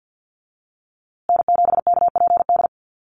Morse code, heard as a keyed tone, about 50 wpm, 700 Hz. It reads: N7KPD
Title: USA Call Signs Repeated 3x 50wpm